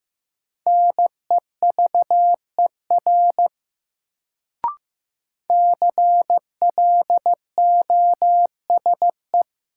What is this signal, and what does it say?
Morse code, heard as a keyed tone, about 15 wpm, 700 Hz. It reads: NEVER CLOSE
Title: Sentences from Top 300 Words - New Word Drill - 15wpm